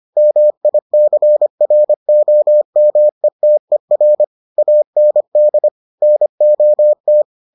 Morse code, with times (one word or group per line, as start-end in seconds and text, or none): 0.2-4.2 MICROMETER
4.6-5.7 AND
6.0-7.2 NOT